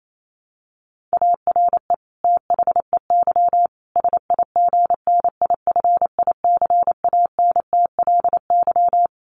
Morse code, heard as a keyed tone, about 28 wpm, 700 Hz. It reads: ARE THEY SIGNIFICANTLY
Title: Sentences from Top 1500 Words 28wpm